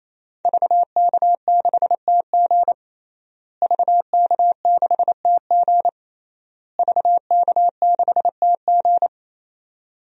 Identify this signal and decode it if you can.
Morse code, heard as a keyed tone, about 28 wpm, 700 Hz. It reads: VK6TG VK6TG VK6TG
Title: DXCC-100 Call Signs - 5 Characters - Repeated 3x - 28wpm